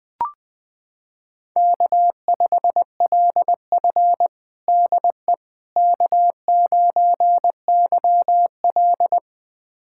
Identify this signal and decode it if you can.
Morse code, heard as a keyed tone, about 20 wpm, 700 Hz. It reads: K5LF DE K9YL
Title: USA Callsign DE Callsign - 4 Characters - 20wpm